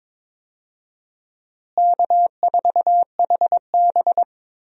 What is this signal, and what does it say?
Morse code, heard as a keyed tone, about 22 wpm, 700 Hz. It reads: K4HB
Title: USA Call Signs - 4 Characters - Repeated 3x - 22wpm